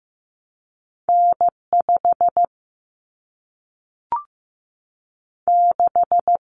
Morse code, heard as a keyed tone, about 15 wpm, 700 Hz. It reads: N5 6
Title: Common Cut Numbers - 2-Digit 15wpm